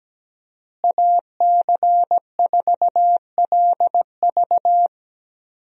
Morse code, heard as a keyed tone, about 17 wpm, 700 Hz. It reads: AC4LV
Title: USA Call Signs - 5 Characters - Repeated 2x - 17wpm